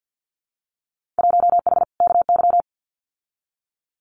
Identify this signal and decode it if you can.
Morse code, heard as a keyed tone, about 50 wpm, 700 Hz. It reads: 1H KY
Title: Winter Field Day Exchanges 50wpm